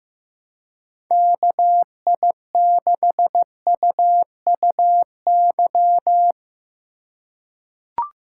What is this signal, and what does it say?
Morse code, heard as a keyed tone, about 15 wpm, 700 Hz. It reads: KI6UUY
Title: USA Call Signs - 6 Characters - 2x Character Spacing - 15wpm